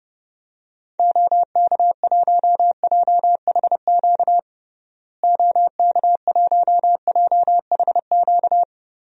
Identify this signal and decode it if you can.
Morse code, heard as a keyed tone, about 30 wpm, 700 Hz. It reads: OK1JHQ OK1JHQ
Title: DXCC-100 Call Signs - 6 Characters - Repeated 2x - 30wpm